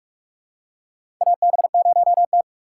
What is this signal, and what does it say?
Morse code, heard as a keyed tone, about 46 wpm, 690 Hz. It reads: AB0T
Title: USA Call Signs - 3x Character Spacing - 45wpm